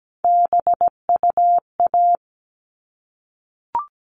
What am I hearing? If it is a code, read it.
Morse code, heard as a keyed tone, about 17 wpm, 700 Hz. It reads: BUA
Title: All Cut Numbers - 3-Digit 17wpm